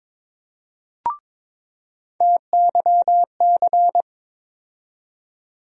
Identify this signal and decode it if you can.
Morse code, heard as a keyed tone, about 21 wpm, 700 Hz. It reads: TYC
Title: USA Call Sign Postfixes - 3 Letters 22wpm